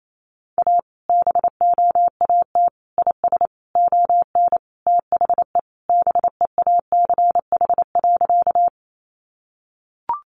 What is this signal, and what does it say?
Morse code, heard as a keyed tone, about 28 wpm, 700 Hz. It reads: A BOAT IS ON THE BEACH.
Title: Sentences from Top 1500 Words 28wpm